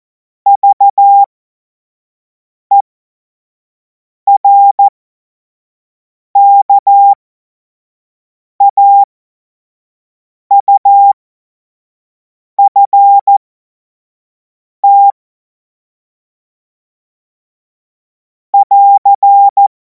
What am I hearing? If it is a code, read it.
Morse code, heard as a keyed tone, about 14 wpm, spaced as for 5 wpm, 800 Hz. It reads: VERKAUFT <AR>
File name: Morsekurs / Witze / Witz 31. März 2026 25 F70